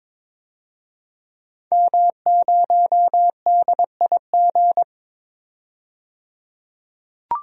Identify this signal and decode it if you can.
Morse code, heard as a keyed tone, about 22 wpm, 700 Hz. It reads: M0DIG
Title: DXCC-100 Call Signs - 5 Characters - 22wpm